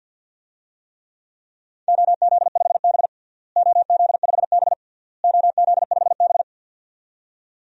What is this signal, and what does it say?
Morse code, heard as a keyed tone, about 51 wpm, 700 Hz. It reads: OZ5B OZ5B OZ5B
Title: DXCC-100 Call Signs - 4 Characters - Repeated 3x - 50wpm